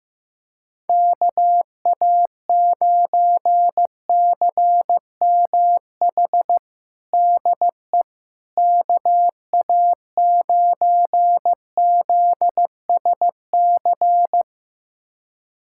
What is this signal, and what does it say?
Morse code, heard as a keyed tone, about 15 wpm, 700 Hz. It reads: KA9CMH DE KA9ZSC
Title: USA Callsign DE Callsign 15wpm